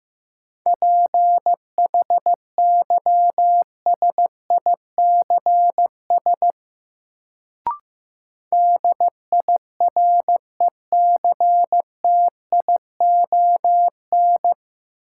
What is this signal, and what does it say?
Morse code, heard as a keyed tone, about 15 wpm, 700 Hz. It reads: PHYSICS DIRECTION
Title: Sentences from Top 1000 Words - Review Part 5 - New Word Drill - 15wpm